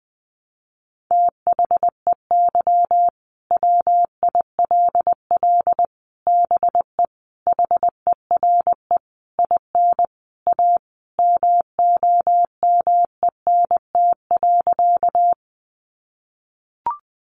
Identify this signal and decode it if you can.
Morse code, heard as a keyed tone, about 20 wpm, 700 Hz. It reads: THEY WILL BE HERE IN A MOMENT.
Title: Sentences from Top 1000 Words 20wpm